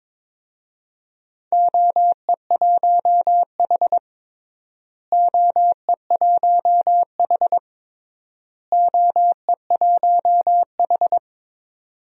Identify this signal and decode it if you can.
Morse code, heard as a keyed tone, about 22 wpm, 700 Hz. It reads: OE1H OE1H OE1H
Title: DXCC-100 Call Signs - 4 Characters - Repeated 3x - 22wpm